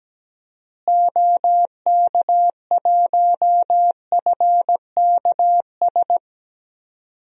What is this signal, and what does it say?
Morse code, heard as a keyed tone, about 17 wpm, 700 Hz. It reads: OK1FKS
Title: DXCC-100 Call Signs - 6 Characters - Repeated 2x - 17wpm